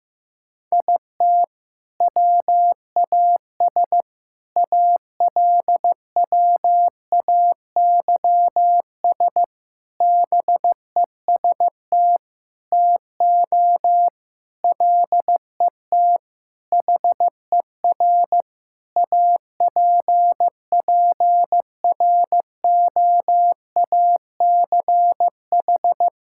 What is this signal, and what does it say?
Morse code, heard as a keyed tone, about 15 wpm, 700 Hz. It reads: IT WAS ALWAYS BEST TO LET HER APPROACH
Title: Sentences from Top 1000 Words - Review Part 6 - 15wpm